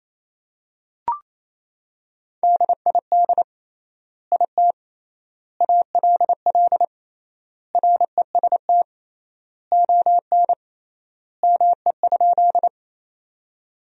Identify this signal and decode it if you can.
Morse code, heard as a keyed tone, about 28 wpm, 700 Hz. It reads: DID IT ALL REST ON ME?
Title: 3x Word-Spacing - Sentences from Top 500 Words - 28wpm